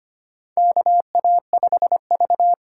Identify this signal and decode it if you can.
Morse code, heard as a keyed tone, about 25 wpm, 700 Hz. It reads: KA5V
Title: USA Call Signs - 4 Characters - 3x Character Spacing - 25wpm